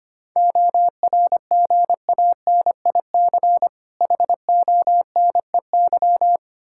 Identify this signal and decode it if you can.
Morse code, heard as a keyed tone, about 25 wpm, 700 Hz. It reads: ORGANIC HONEY
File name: practice / 25wpm / Difficult Words with Hints - 25wpm